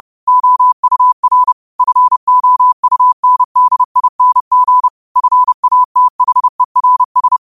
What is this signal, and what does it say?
Morse code, heard as a keyed tone, about 30 wpm, 1000 Hz. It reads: OUR FOUNDING FATHERS